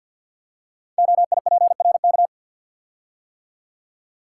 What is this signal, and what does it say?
Morse code, heard as a keyed tone, about 53 wpm, 700 Hz. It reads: OIPRK
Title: Code Groups - 5 Letters - 50wpm